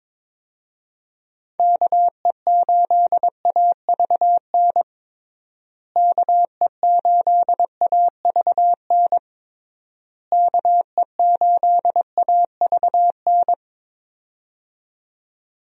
Morse code, heard as a keyed tone, about 22 wpm, 700 Hz. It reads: KE8AVN KE8AVN KE8AVN
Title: USA Call Signs - 6 Characters - Repeated 3x - 22wpm